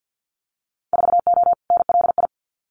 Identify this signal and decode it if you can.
Morse code, heard as a keyed tone, about 50 wpm, 700 Hz. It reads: 4O NLI
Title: Winter Field Day Exchanges Encore 50wpm